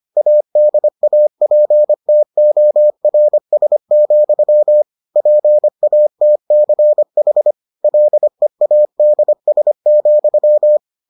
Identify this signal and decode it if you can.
Morse code, heard as a keyed tone, about 25 wpm, 600 Hz. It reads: ADAPTORS, PATCH LEADS,